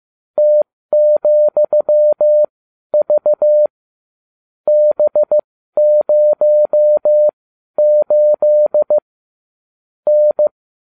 Morse code, heard as a keyed tone, about 15 wpm, spaced as for 12 wpm, 600 Hz. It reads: T,V B08 N